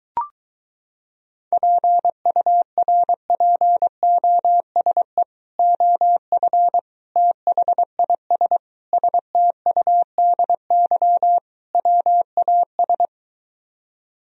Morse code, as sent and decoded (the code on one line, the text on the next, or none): .--. ..- .-. .--. --- ... / --- ..-. / - .... .. ... / ... - ..- -.. -.-- / .-- .- ...
PURPOSE OF THIS STUDY WAS